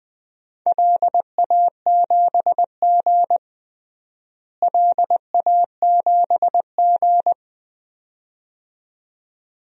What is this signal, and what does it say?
Morse code, heard as a keyed tone, about 20 wpm, 700 Hz. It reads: LA7G LA7G
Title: DXCC-100 Call Signs - 4 Characters - Repeated 2x - 20wpm